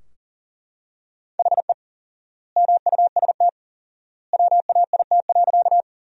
Morse code, heard as a keyed tone, about 40 wpm, 700 Hz. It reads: HE MUST WAIT.
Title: 4x Word-Spacing - Sentences from Top 500 Words - 40wpm